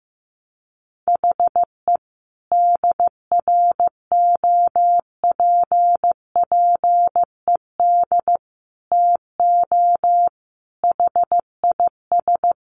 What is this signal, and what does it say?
Morse code, heard as a keyed tone, about 15 wpm, 700 Hz. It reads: HE DROPPED TO HIS
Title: A Sound Of Thunder 15wpm